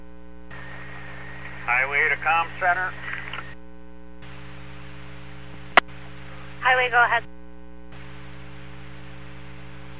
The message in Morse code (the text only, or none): none